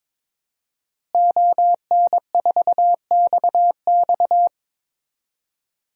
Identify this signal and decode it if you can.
Morse code, heard as a keyed tone, about 22 wpm, 700 Hz. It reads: ON4XX
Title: DXCC-100 Call Signs - 5 Characters - Repeated 2x - 22wpm